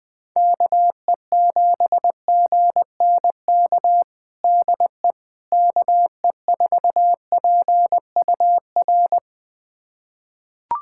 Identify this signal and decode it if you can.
Morse code, heard as a keyed tone, about 20 wpm, 700 Hz. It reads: KE7GNK DE KE4PUR
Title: USA Callsign DE Callsign - 6 Characters - 20wpm